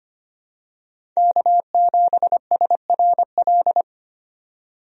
Morse code, heard as a keyed tone, about 25 wpm, 700 Hz. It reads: K7SRL
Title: USA Call Signs - 5 Characters - 25wpm